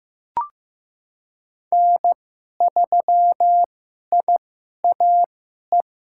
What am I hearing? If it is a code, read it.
Morse code, heard as a keyed tone, about 15 wpm, spaced as for 11 wpm, 700 Hz. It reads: N3IAE
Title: USA Call Signs - 5 Characters - 2x Character Spacing - 15wpm